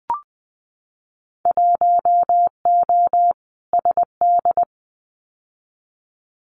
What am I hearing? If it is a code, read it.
Morse code, heard as a keyed tone, about 20 wpm, 700 Hz. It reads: 1O SD
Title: Winter Field Day Exchanges Encore 20wpm